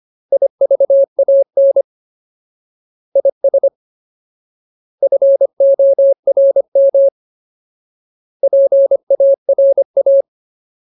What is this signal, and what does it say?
Morse code, heard as a keyed tone, about 25 wpm, 550 Hz. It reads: IVAN IS FORM PARA
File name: Ultimate Word List / 25WPM Ultimate